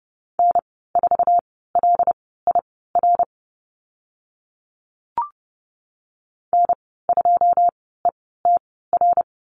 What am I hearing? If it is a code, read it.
Morse code, heard as a keyed tone, about 30 wpm, spaced as for 17 wpm, 700 Hz. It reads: N4LIR N2ETR
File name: course / 30:10wpm / 088 Lesson 15g - Number 5 - Call Signs 30wpm